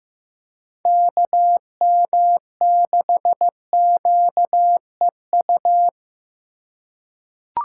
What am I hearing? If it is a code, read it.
Morse code, heard as a keyed tone, about 15 wpm, 700 Hz. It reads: KM6QEU
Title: USA Call Signs - 6 Characters - 4x Character Spacing - 15wpm